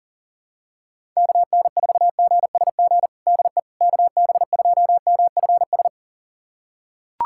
Morse code, heard as a keyed tone, about 40 wpm, 700 Hz. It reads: KN4GSG DE KB2MFS
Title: USA Callsign DE Callsign - 6 Characters - 40wpm